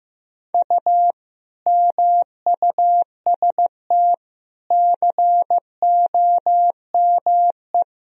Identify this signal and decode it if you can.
Morse code, heard as a keyed tone, about 15 wpm, 700 Hz. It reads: U MUST COME
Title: Sentences from Top 200 Words and Abbreviations 15wpm